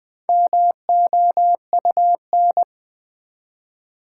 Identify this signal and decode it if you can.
Morse code, heard as a keyed tone, about 20 wpm, 700 Hz. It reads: MOUN
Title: Build A Word - 1 Letter At A Time - 3 to 5-Letter Words - 20wpm